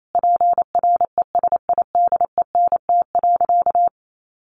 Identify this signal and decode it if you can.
Morse code, heard as a keyed tone, about 28 wpm, 700 Hz. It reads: PRESIDENT.
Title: Sentences from Top 1000 Words 28wpm